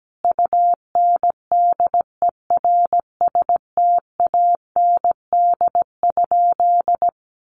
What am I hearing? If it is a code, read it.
Morse code, heard as a keyed tone, about 17 wpm, 700 Hz. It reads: UNDERSTAND?
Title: Sentences from Top 1500 Words 17wpm